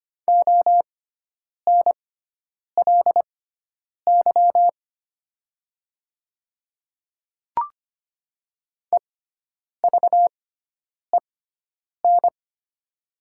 Morse code, heard as a keyed tone, about 25 wpm, spaced as for 9 wpm, 700 Hz. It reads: ONLY EVEN
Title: 6x Character Spacing - Top 300 Words - 25wpm